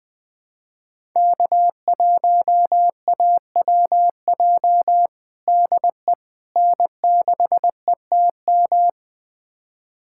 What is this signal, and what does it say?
Morse code, heard as a keyed tone, about 20 wpm, 700 Hz. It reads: K1AWJ DE N6ETM
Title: USA Callsign DE Callsign 20wpm